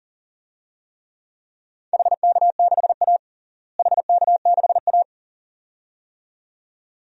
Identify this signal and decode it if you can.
Morse code, heard as a keyed tone, about 40 wpm, 700 Hz. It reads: HK6A HK6A
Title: DXCC-100 Call Signs - 4 Characters - Repeated 2x - 40wpm